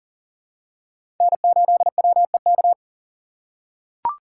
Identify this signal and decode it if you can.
Morse code, heard as a keyed tone, about 40 wpm, 700 Hz. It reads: N8WEK
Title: USA Call Signs - 5 Characters - 2x Character Spacing - 40wpm